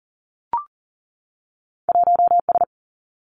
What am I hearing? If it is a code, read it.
Morse code, heard as a keyed tone, about 43 wpm, 700 Hz. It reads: 1S
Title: Two Challenging Letter-Number Combinations 40wpm